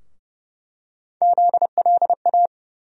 Morse code, heard as a keyed tone, about 30 wpm, 700 Hz. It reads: ZLA
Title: USA Call Sign Postfixes - 3 Letters 30wpm